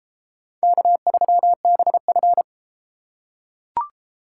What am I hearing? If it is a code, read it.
Morse code, heard as a keyed tone, about 29 wpm, 700 Hz. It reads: K3BF